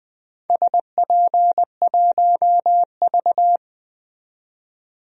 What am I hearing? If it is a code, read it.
Morse code, heard as a keyed tone, about 20 wpm, 700 Hz. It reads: SP1V
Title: DXCC-100 Call Signs - 4 Characters - 20wpm